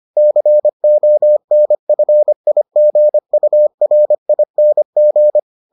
Morse code, heard as a keyed tone, about 25 wpm, 600 Hz. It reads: CONFIGURING